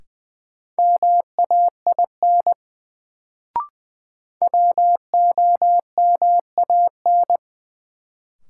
Morse code, heard as a keyed tone, about 20 wpm, 700 Hz. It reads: MAIN WOMAN
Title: Sentences from Top 1000 Words - Review Part 2 - New Word Drill - 20wpm